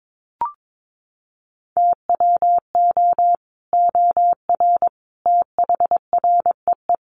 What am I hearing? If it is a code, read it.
Morse code, heard as a keyed tone, about 22 wpm, 700 Hz. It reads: TWO OR THREE
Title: Sets of 5 Words Encore2 22wpm